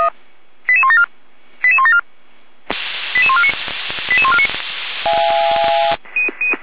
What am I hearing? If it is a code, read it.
Five-tone selective call (ZVEI), heard as quick runs of five tones, tones of about 70 ms. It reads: 88174 88174 88149 88149
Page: /455/SDR/ZVEI/88124/